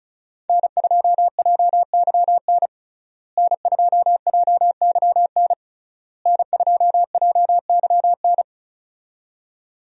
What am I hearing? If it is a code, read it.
Morse code, heard as a keyed tone, about 36 wpm, 700 Hz. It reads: N2JYN N2JYN N2JYN
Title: USA Call Signs - 5 Characters - Repeated 3x - 35wpm